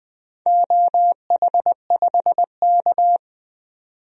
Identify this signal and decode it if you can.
Morse code, heard as a keyed tone, about 20 wpm, 700 Hz. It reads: OH5K
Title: DXCC-100 Call Signs - 4 Characters - 20wpm